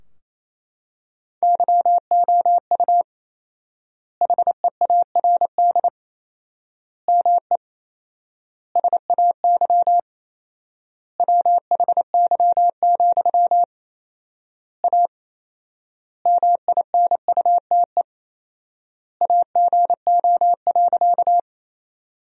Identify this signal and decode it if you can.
Morse code, heard as a keyed tone, about 27 wpm, 700 Hz. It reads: YOU HEARD ME SAY WHY, A MINUTE AGO.
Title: 4x Word-Spacing - Sentences from Top 500 Words - 28wpm